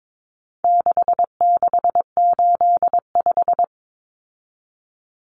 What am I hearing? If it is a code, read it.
Morse code, heard as a keyed tone, about 22 wpm, 700 Hz. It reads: 6685
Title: Common Cut Numbers - 4-Digit 22wpm